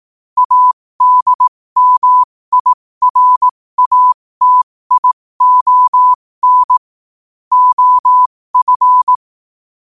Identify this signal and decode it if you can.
Morse code, heard as a keyed tone, about 18 wpm, spaced as for 15 wpm, 1000 Hz. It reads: ADMIRATION OF
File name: qotdfiles / 15wpm